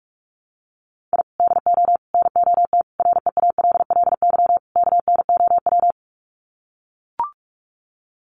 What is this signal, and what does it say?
Morse code, heard as a keyed tone, about 44 wpm, 700 Hz. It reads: I DO NOT REALLY KNOW
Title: Sets of 5 Words Encore 45wpm